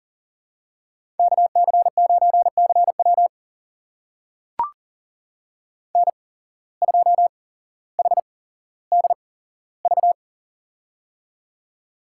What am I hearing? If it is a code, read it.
Morse code, heard as a keyed tone, about 37 wpm, spaced as for 17 wpm, 700 Hz. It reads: KC9CW N2HDV
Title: USA Call Signs - 5 Characters - 8x Character Spacing - 40wpm